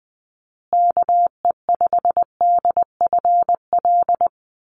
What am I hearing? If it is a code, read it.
Morse code, heard as a keyed tone, about 20 wpm, 700 Hz. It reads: KE5DFL
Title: USA Call Signs Repeated 3x 20wpm